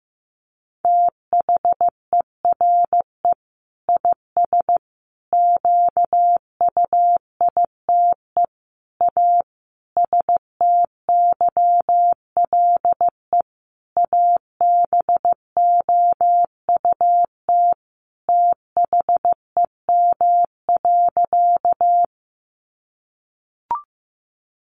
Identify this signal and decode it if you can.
Morse code, heard as a keyed tone, about 15 wpm, 700 Hz. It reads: THERE IS QUITE A STYLE ABOUT THEM.
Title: Sentences from Top 800 Words 15wpm